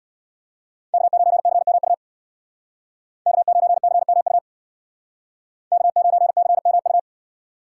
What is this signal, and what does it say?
Morse code, heard as a keyed tone, about 63 wpm, 700 Hz. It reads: K9CMU K9CMU K9CMU